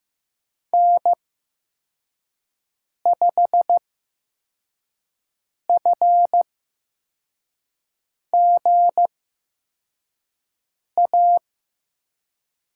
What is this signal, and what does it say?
Morse code, heard as a keyed tone, about 15 wpm, spaced as for 4 wpm, 700 Hz. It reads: N5FGA
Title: USA Call Signs - 5 Characters - 8x Character Spacing - 15wpm